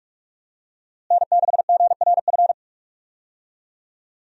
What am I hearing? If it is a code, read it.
Morse code, heard as a keyed tone, about 41 wpm, 700 Hz. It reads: N6GRF
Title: USA Call Signs - 5 Characters - Repeated 3x - 45wpm